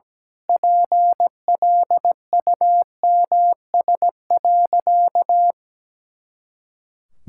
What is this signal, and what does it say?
Morse code, heard as a keyed tone, about 17 wpm, 700 Hz. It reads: PLUMS.
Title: Pangram Challenge v2 - 17wpm